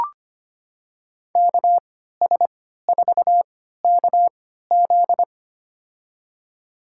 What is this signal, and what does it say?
Morse code, heard as a keyed tone, about 25 wpm, spaced as for 14 wpm, 700 Hz. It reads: KS4KZ
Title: USA Call Signs - 5 Characters - 3x Character Spacing - 25wpm